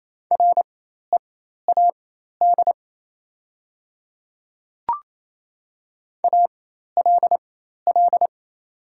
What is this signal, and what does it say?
Morse code, heard as a keyed tone, about 28 wpm, spaced as for 13 wpm, 700 Hz. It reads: READ ALL
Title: 4x Character Spacing - Top 300 Words - 28wpm